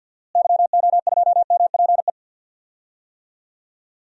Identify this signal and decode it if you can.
Morse code, heard as a keyed tone, about 50 wpm, 700 Hz. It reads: YO2MPE
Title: DXCC-100 Call Signs - 6 Characters - Repeated 2x - 50wpm